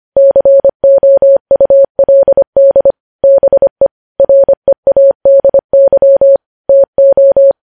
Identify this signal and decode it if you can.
Morse code, heard as a keyed tone, about 25 wpm, 550 Hz. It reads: COULD BE READY TO
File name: podcast / News headlines-1x-25x25